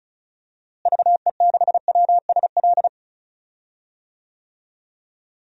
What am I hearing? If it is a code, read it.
Morse code, heard as a keyed tone, about 34 wpm, 700 Hz. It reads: VE6WSL
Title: DXCC-100 Call Signs - 6 Characters - Repeated 2x - 35wpm